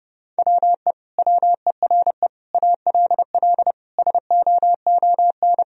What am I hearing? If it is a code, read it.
Morse code, heard as a keyed tone, about 30 wpm, 700 Hz. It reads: WE WERE ALL SOON
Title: Sentences from Top 1000 Words - Review Part 4 - 30wpm